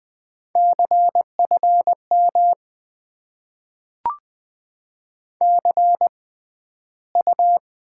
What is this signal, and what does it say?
Morse code, heard as a keyed tone, about 21 wpm, spaced as for 12 wpm, 700 Hz. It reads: CFM CU
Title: Top 90 QSO Elements - 6x Character Spacing - 20wpm